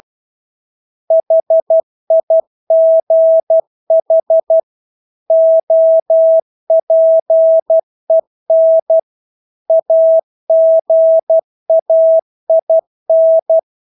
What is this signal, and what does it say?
Morse code, heard as a keyed tone, about 12 wpm, 650 Hz. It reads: HIGH OPEN AGAIN
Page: Engelske ord 12 wpm